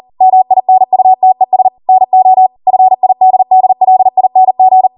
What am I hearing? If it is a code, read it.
Morse code, heard as a keyed tone, about 40 wpm, 750 Hz. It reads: MINUTES NO FIDDLING